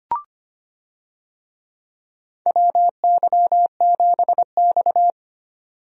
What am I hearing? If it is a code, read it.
Morse code, heard as a keyed tone, about 25 wpm, 700 Hz. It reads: WY7X
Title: USA Call Signs - 4 Characters - Repeated 3x - 25wpm